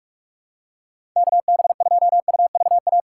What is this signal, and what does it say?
Morse code, heard as a keyed tone, about 44 wpm, 700 Hz. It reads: KB2UVA
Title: USA Call Signs - 2x Character Spacing - 45wpm